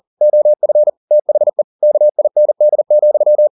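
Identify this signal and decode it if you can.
Morse code, heard as a keyed tone, about 40 wpm, 600 Hz. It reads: OF THE KIND,
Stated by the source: Morse code at 40 wpm, practice text, CW med Gnister.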